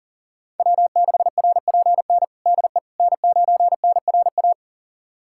Morse code, heard as a keyed tone, about 40 wpm, 700 Hz. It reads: W6RPN DE N9NRA
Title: USA Callsign DE Callsign - 5 Characters - 40wpm